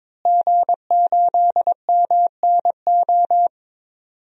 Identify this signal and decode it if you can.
Morse code, heard as a keyed tone, about 21 wpm, 700 Hz. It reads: G8MNO
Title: DXCC-100 Call Signs - 5 Characters - Repeated 2x - 22wpm